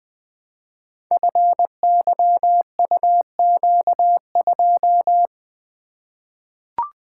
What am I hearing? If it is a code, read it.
Morse code, heard as a keyed tone, about 20 wpm, 700 Hz. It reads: FYUQ2
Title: Code Groups - 5 Letter:Numbers - 20wpm